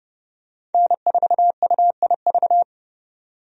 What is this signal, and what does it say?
Morse code, heard as a keyed tone, about 32 wpm, 700 Hz. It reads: N4UIV
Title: USA Call Signs - 5 Characters - 30wpm